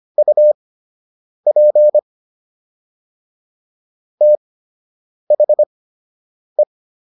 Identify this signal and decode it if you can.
Morse code, heard as a keyed tone, about 25 wpm, spaced as for 8 wpm, 600 Hz. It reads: UP THE